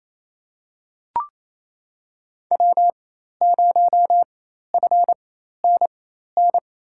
Morse code, heard as a keyed tone, about 27 wpm, spaced as for 13 wpm, 700 Hz. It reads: W0FNN
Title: USA Call Signs - 5 Characters - 4x Character Spacing - 28wpm